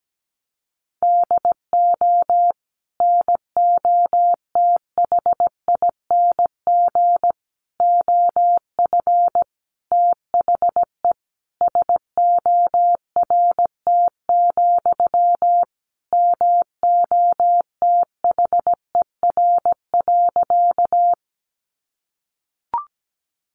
Morse code, heard as a keyed tone, about 17 wpm, 700 Hz. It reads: DO NOTHING OF THE SORT, MOTHER.
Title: Sentences from Top 1300 Words 17wpm